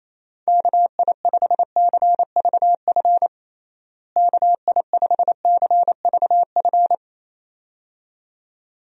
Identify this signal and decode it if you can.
Morse code, heard as a keyed tone, about 28 wpm, 700 Hz. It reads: KI5CVF KI5CVF
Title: USA Call Signs - 6 Characters - Repeated 2x - 28wpm